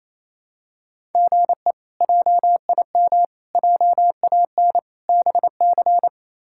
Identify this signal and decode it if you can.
Morse code, heard as a keyed tone, about 28 wpm, 700 Hz. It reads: GE JIM JAN BC
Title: K1USN SST QSO Practice - 28wpm